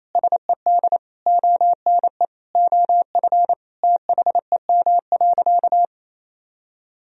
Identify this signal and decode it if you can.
Morse code, heard as a keyed tone, about 28 wpm, 700 Hz. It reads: SED ONE OF THEM.
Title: Sentences from Top 300 Words and Abbreviations 28wpm